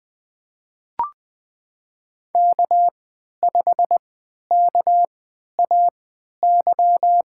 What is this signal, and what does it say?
Morse code, heard as a keyed tone, about 20 wpm, spaced as for 11 wpm, 700 Hz. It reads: K5KAY
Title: USA Call Signs - 5 Characters - 3x Character Spacing - 20wpm